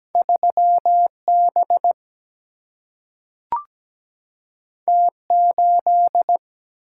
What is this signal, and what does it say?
Morse code, heard as a keyed tone, about 17 wpm, 700 Hz. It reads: 3 B T 8
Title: Single-Letter Single-Number 17wpm